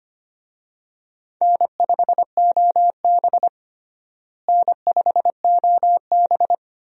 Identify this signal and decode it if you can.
Morse code, heard as a keyed tone, about 25 wpm, 700 Hz. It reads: N5OB N5OB
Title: USA Call Signs - 4 Characters - Repeated 2x - 25wpm